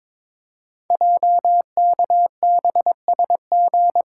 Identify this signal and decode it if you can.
Morse code, heard as a keyed tone, about 22 wpm, 700 Hz. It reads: JKBSG
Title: Code Groups - 5 Letters - 22wpm